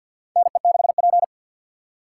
Morse code, heard as a keyed tone, about 52 wpm, 700 Hz. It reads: NE6P
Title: USA Call Signs - 4 Characters - 2x Character Spacing - 50wpm